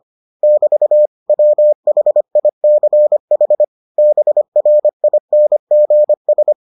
Morse code, heard as a keyed tone, about 25 wpm, 600 Hz. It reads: <BT> WHICH BRINGS